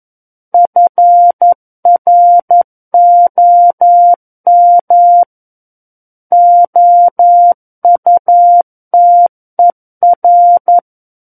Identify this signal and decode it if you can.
Morse code, heard as a keyed tone, about 11 wpm, 700 Hz. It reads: FROM OUTER
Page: SKCC - War of the Worlds - 02-Chapter 2 - 11 WPM